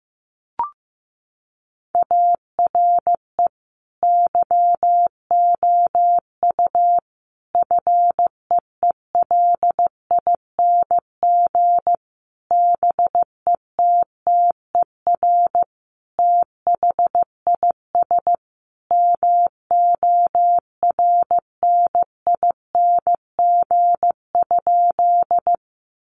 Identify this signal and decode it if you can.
Morse code, heard as a keyed tone, about 15 wpm, 700 Hz. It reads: ARE YOU FEELING BETTER THIS MORNING?
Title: Sentences from Top 1200 Words 15wpm